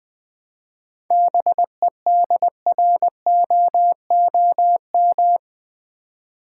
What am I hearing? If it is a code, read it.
Morse code, heard as a keyed tone, about 20 wpm, 700 Hz. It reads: BEDROOM
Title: 7 Letter Words - First 3 Letters Given - 20wpm